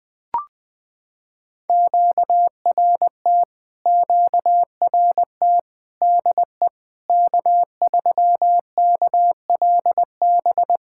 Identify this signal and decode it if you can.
Morse code, heard as a keyed tone, about 20 wpm, 700 Hz. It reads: QRT QRT DE K3KLB
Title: POTA QSO Practice - 20wpm